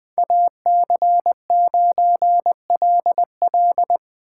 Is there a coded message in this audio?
Yes: Morse code from 0.2 to 4.0 s, heard as a keyed tone, about 20 wpm, 700 Hz. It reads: AC9LL